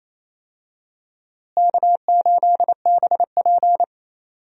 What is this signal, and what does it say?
Morse code, heard as a keyed tone, about 28 wpm, 700 Hz. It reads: K8BP
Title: USA Call Signs - 4 Characters - Repeated 3x - 28wpm